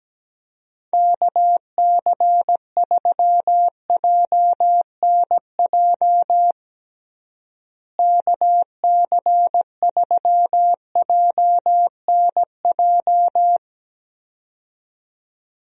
Morse code, heard as a keyed tone, about 17 wpm, 700 Hz. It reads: KC3JNJ KC3JNJ
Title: USA Call Signs - 6 Characters - Repeated 2x - 17wpm